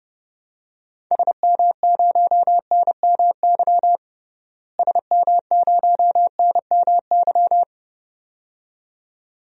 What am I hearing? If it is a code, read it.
Morse code, heard as a keyed tone, about 29 wpm, 700 Hz. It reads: SM0NMY SM0NMY
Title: DXCC-100 Call Signs - 6 Characters - Repeated 2x - 30wpm